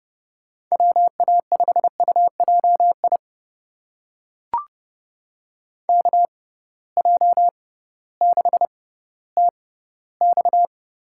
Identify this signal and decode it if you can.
Morse code, heard as a keyed tone, about 30 wpm, spaced as for 16 wpm, 700 Hz. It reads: WA5UJI KJ6TX
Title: USA Call Signs - 6 Characters - 6x Character Spacing - 30wpm